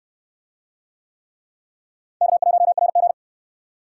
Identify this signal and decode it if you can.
Morse code, heard as a keyed tone, about 65 wpm, 700 Hz. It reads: K1RG